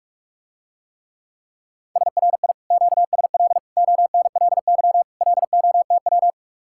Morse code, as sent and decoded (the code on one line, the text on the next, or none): ... .-. .. / --.- ... .-.. / --- -. .-.. -.-- / .-.. --- - .--
SRI QSL ONLY LOTW